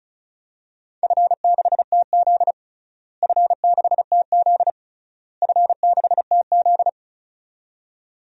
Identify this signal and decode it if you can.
Morse code, heard as a keyed tone, about 34 wpm, 700 Hz. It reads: F6TZ F6TZ F6TZ
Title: DXCC-100 Call Signs - 4 Characters - Repeated 3x - 35wpm